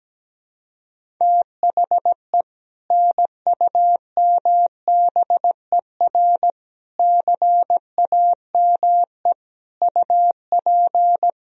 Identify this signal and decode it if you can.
Morse code, heard as a keyed tone, about 17 wpm, 700 Hz. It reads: THE NUMBER CAME UP
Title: Sentences from Top 1000 Words - Review Part 1 - 17wpm